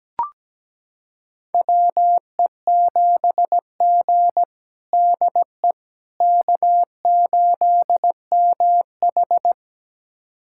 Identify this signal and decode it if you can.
Morse code, heard as a keyed tone, about 17 wpm, 700 Hz. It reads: WE7G DE K8MH
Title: USA Callsign DE Callsign - 4 Characters - 17wpm